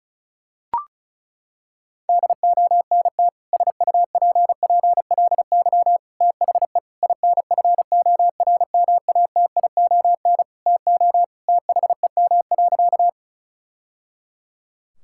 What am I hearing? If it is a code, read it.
Morse code, heard as a keyed tone, about 36 wpm, 700 Hz. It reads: DONT SUPPLY THE INFORMATION TO THEM.
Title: Sentences from Top 1000 Words - Review Part 5 - 35wpm